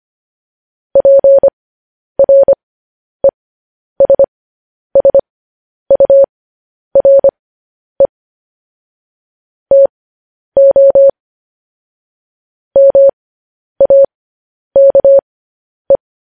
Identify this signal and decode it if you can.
Morse code, heard as a keyed tone, about 25 wpm, spaced as for 10 wpm, 550 Hz. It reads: PRESSURE TO MAKE
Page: News Headlines in Morse Code at 10 WPM